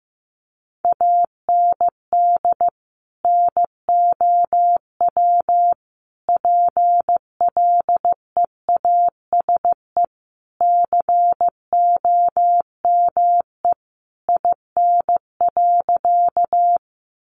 Morse code, heard as a keyed tone, about 15 wpm, 700 Hz. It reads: AND NOW PLEASE COME IN.
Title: Sentences from Top 1500 Words 15wpm